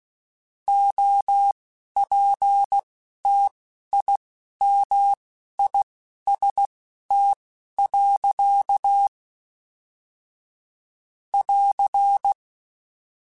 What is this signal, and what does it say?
Morse code, heard as a keyed tone, about 16 wpm, spaced as for 11 wpm, 800 Hz. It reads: OPTIMIST. <AR>
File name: Quotes / Quotes-mp3s / Optimist @10